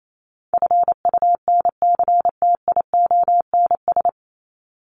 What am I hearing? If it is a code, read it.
Morse code, heard as a keyed tone, about 28 wpm, 700 Hz. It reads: FUNCTIONS
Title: ES:S Suffix - Top 100 Words with Base Words 28wpm